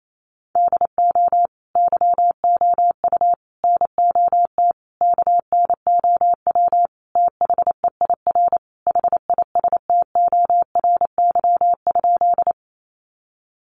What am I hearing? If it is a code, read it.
Morse code, heard as a keyed tone, about 28 wpm, 700 Hz. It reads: DO YOU NOT KNOW THEIR HISTORY?
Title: Sentences from Top 600 Words 28wpm